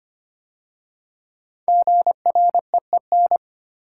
Morse code, heard as a keyed tone, about 25 wpm, 700 Hz. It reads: GREEN
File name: 5 Letter Words - First 3 Letters Given - 25wpm